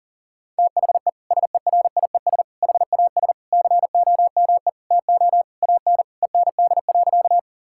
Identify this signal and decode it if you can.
Morse code, heard as a keyed tone, about 40 wpm, 700 Hz. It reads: THE SERIES HAS COME TO AN END.
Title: Sentences from Top 1000 Words - Review Part 5 - 40wpm